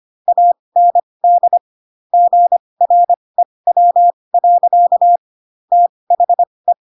Morse code, heard as a keyed tone, about 25 wpm, spaced as for 20 wpm, 700 Hz. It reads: AND GREW. THE